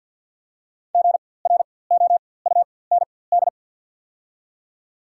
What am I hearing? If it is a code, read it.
Morse code, heard as a keyed tone, about 54 wpm, spaced as for 24 wpm, 690 Hz. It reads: GROUND
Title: 4x Character Spacing - Top 500 Words - 50wpm